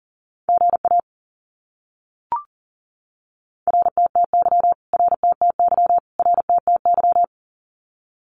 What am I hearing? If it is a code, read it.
Morse code, heard as a keyed tone, about 40 wpm, 700 Hz. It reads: GA RTTY RTTY RTTY
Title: Abbreviations Repeated 3x - 40wpm